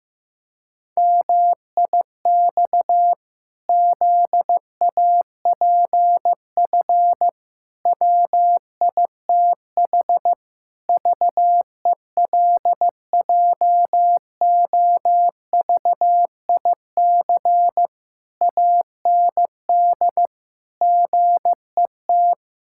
Morse code, heard as a keyed tone, about 15 wpm, 700 Hz. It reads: MIX ZAPF WITH VELJOVIC AND GET
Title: Pangram Challenge v2 - 15wpm